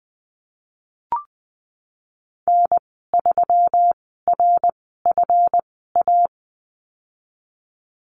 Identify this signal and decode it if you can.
Morse code, heard as a keyed tone, about 19 wpm, spaced as for 14 wpm, 700 Hz. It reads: N3RFA
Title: 167 Lesson 27g - Number 7 - Call Signs 20wpm